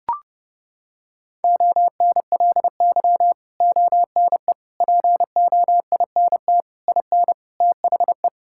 Morse code, heard as a keyed tone, about 30 wpm, 700 Hz. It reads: ONLY ONE POINT IN THE
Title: Sentences from Top 1000 Words - Review Part 5 - 30wpm